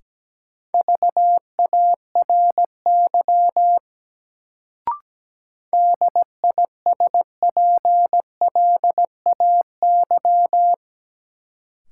Morse code, heard as a keyed tone, about 17 wpm, 700 Hz. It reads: VARY DISPLAY
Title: Sentences from Top 1200 Words - New Word Drill - 17wpm